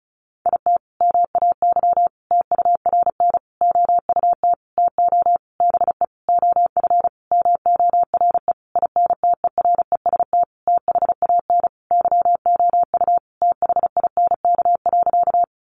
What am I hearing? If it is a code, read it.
Morse code, heard as a keyed tone, about 34 wpm, 700 Hz. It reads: IT MAY TURN OUT TO BE OF MORE INTEREST THAN YOU THINK.
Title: Sentences from Top 400 Words 35wpm